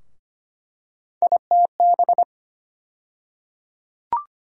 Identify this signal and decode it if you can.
Morse code, heard as a keyed tone, about 25 wpm, 700 Hz. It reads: ITB
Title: USA Call Sign Postfixes - 3 Letters 25wpm